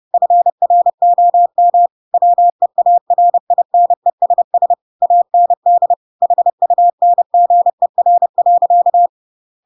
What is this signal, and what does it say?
Morse code, heard as a keyed tone, about 30 wpm, 700 Hz. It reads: FROM WEARINESS AND HUNGER.
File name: Pinocchio / Pinocchio ch6 30 30 ik1hge